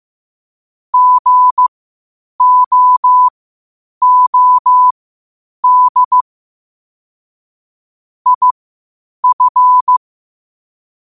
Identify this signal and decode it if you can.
Morse code, heard as a keyed tone, about 15 wpm, spaced as for 8 wpm, 1000 Hz. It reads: GOOD IF
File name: qotdfiles / 8wpm